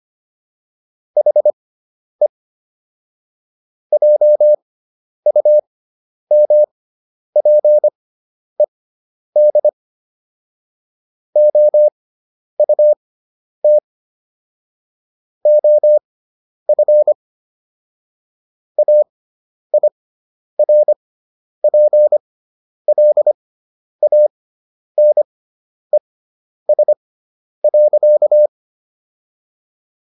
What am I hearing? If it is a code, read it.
Morse code, heard as a keyed tone, about 25 wpm, spaced as for 10 wpm, 600 Hz. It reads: HE JUMPED OUT OF AIRPLANES.